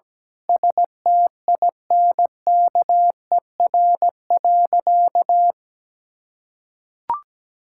Morse code, heard as a keyed tone, about 17 wpm, 700 Hz. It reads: STINKER.
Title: Pangram Challenge v2 - 17wpm